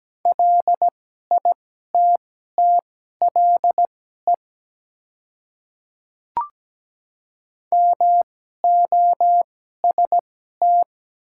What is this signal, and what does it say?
Morse code, heard as a keyed tone, about 17 wpm, spaced as for 12 wpm, 700 Hz. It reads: LITTLE MOST
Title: 2x Character Spacing - Top 300 Words - 17wpm